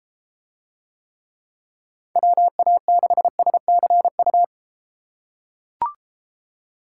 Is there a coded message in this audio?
Yes: Morse code from 2.2 to 4.4 s, heard as a keyed tone, about 32 wpm, 700 Hz. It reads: WA6SCU